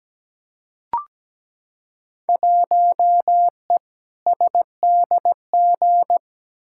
Morse code, heard as a keyed tone, about 17 wpm, 700 Hz. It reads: 1E SDG
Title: Field Day Exchanges - 17wpm